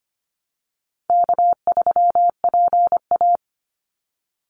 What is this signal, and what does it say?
Morse code, heard as a keyed tone, about 25 wpm, 700 Hz. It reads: K3PA
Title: Easy Call Signs - 25wpm